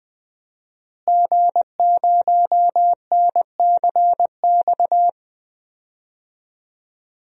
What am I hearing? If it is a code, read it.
Morse code, heard as a keyed tone, about 21 wpm, 700 Hz. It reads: G0NCX
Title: DXCC-100 Call Signs - 5 Characters - Repeated 3x - 20wpm